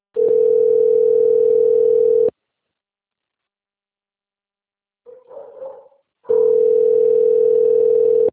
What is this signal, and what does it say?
Ringback tone (North America), 440 + 480 Hz.